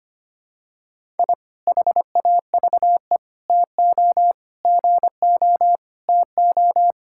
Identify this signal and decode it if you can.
Morse code, heard as a keyed tone, about 25 wpm, 700 Hz. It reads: I HAVE TO GO TO
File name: practice / 25wpm / Sentences with 3-Word Phrases - 25wpm